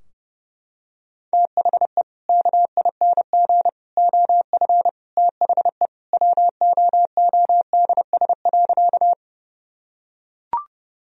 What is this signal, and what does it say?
Morse code, heard as a keyed tone, about 32 wpm, 700 Hz. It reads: THE KING OF THE WOODS.
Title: Sentences from Top 1000 Words - Review Part 2 - 30wpm